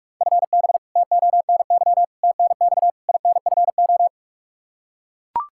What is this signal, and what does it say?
Morse code, heard as a keyed tone, about 45 wpm, 700 Hz. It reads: FB TONY TNX INFO
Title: Practice QSO - Exchange 3 - More Info - 45wpm